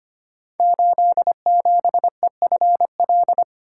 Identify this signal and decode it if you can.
Morse code, heard as a keyed tone, about 25 wpm, 700 Hz. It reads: 87EFL
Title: Code Groups - 5 Letter:Numbers - 25wpm